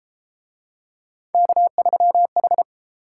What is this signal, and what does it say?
Morse code, heard as a keyed tone, about 31 wpm, 700 Hz. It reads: K3H